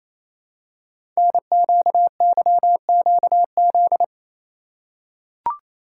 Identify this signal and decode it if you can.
Morse code, heard as a keyed tone, about 28 wpm, 700 Hz. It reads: NQYQZ
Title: Code Groups - 5 Letters - 28wpm